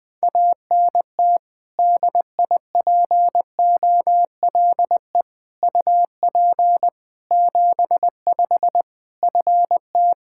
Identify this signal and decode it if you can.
Morse code, heard as a keyed tone, about 21 wpm, 700 Hz. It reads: ANT DIPOLE UP 75 FT
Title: Practice QSO - Exchange 2 - Info - 20wpm